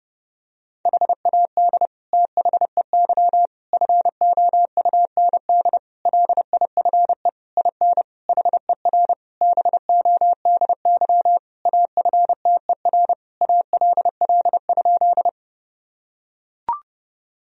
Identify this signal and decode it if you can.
Morse code, heard as a keyed tone, about 32 wpm, 700 Hz. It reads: HAD THEY FOUND LIFE IN HER BODY AFTER ALL?
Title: Sentences from Top 1000 Words - Review Part 2 - 30wpm